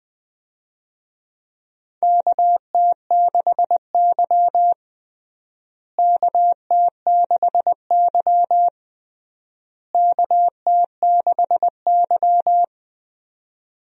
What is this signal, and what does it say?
Morse code, heard as a keyed tone, about 20 wpm, 700 Hz. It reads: KT6Y KT6Y KT6Y
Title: USA Call Signs - 4 Characters - Repeated 3x - 20wpm